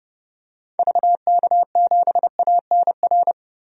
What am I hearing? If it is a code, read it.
Morse code, heard as a keyed tone, about 30 wpm, 700 Hz. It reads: VK7ANR
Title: DXCC-100 Call Signs - 6 Characters - Repeated 2x - 30wpm